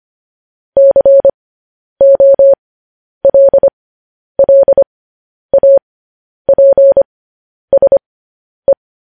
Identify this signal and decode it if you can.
Morse code, heard as a keyed tone, about 25 wpm, spaced as for 10 wpm, 550 Hz. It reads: COLLAPSE
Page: news headlines in morse code at 10 wpm